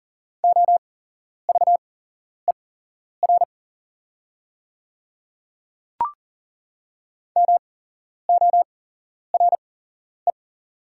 Morse code, heard as a keyed tone, about 40 wpm, spaced as for 11 wpm, 700 Hz. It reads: OVER MORE
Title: 8x Character Spacing - Top 100 Words - 40wpm